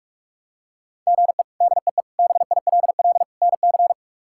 Morse code, heard as a keyed tone, about 44 wpm, 700 Hz. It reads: GE DEE BILL NC
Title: K1USN SST QSO Practice - 45wpm